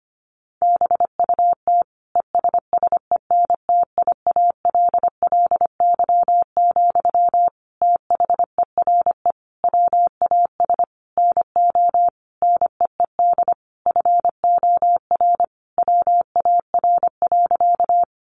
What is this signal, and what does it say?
Morse code, heard as a keyed tone, about 25 wpm, 700 Hz. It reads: BUT ESSENTIALLY, THERE WAS NO NEED FOR WAR.
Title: Sentences from Top 1300 Words 25wpm